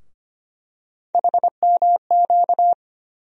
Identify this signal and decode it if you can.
Morse code, heard as a keyed tone, about 25 wpm, 700 Hz. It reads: HMQ
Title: USA Call Sign Postfixes - 3 Letters 25wpm